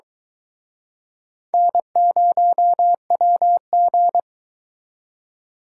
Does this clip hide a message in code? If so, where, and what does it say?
Yes: Morse code from 1.5 to 4.2 s, heard as a keyed tone, about 24 wpm, 700 Hz. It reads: N0WG